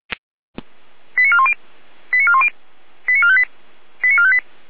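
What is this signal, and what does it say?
Five-tone selective call (ZVEI), heard as quick runs of five tones, tones of about 70 ms. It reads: 89410 89410 89458 89458
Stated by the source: ZVEI five-tone selective call 89320 received by SDR.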